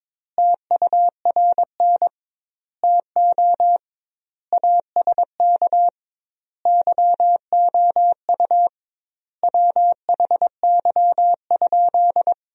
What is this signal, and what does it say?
Morse code, heard as a keyed tone, about 22 wpm, 700 Hz. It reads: TURN TO ASK YOU WHY?
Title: 2x Word-Spacing - Sentences from Top 200 Words - 22wpm